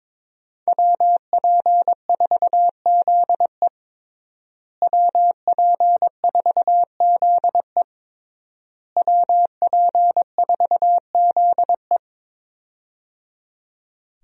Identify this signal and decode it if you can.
Morse code, heard as a keyed tone, about 21 wpm, 700 Hz. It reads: WP4ZE WP4ZE WP4ZE
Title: USA Call Signs - 5 Characters - Repeated 3x - 22wpm